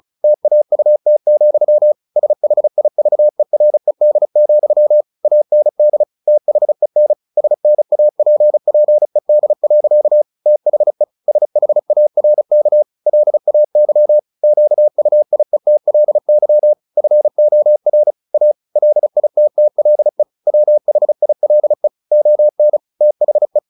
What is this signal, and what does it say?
Morse code, heard as a keyed tone, about 34 wpm, 600 Hz. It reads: TAUT, SHIVERED, AND THEN SNAPPED. THE SHARK LAY QUIETLY FOR A LITTLE WHILE ON THE
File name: the old man and the sea 2 35wpm